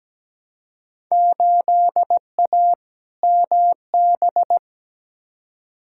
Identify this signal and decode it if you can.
Morse code, heard as a keyed tone, about 17 wpm, 700 Hz. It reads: 8A MB
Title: Field Day Exchanges Encore - 17wpm